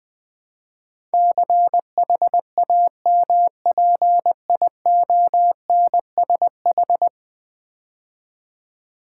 Morse code, heard as a keyed tone, about 20 wpm, 700 Hz. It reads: CHAMPIONSH
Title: Build A Word - 5 Letters At A Time - 12-Letter Words - 20wpm